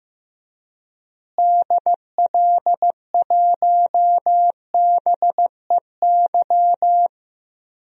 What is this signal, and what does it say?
Morse code, heard as a keyed tone, about 15 wpm, 700 Hz. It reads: DL1BEY
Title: DXCC-100 Call Signs - 6 Characters - Repeated 2x - 15wpm